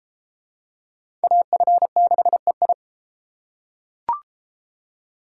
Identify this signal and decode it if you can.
Morse code, heard as a keyed tone, about 32 wpm, 700 Hz. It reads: AF6EI